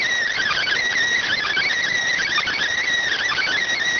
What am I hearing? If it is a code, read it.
Emergency Alert System (SAME) tones, heard as harsh buzzing bursts. It is heard from 0.3 to 3.7 s.